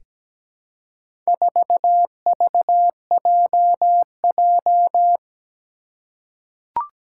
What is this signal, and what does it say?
Morse code, heard as a keyed tone, about 17 wpm, 700 Hz. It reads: 4VJJ
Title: Four Challenging Letter-Number Combinations 17wpm